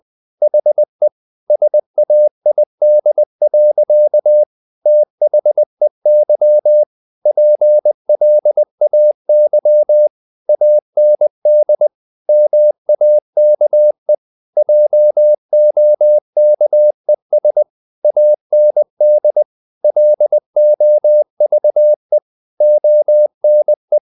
Morse code, heard as a keyed tone, about 20 wpm, 600 Hz. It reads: HE SAID. THEY PLAY AND MAKE JOKES AND LOVE ONE
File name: the old man and the sea / the old man and the sea part 1 20wpm